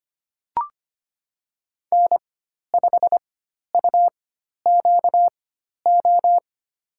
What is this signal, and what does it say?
Morse code, heard as a keyed tone, about 25 wpm, spaced as for 12 wpm, 700 Hz. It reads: N5UQO
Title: USA Call Signs - 4x Character Spacing - 25wpm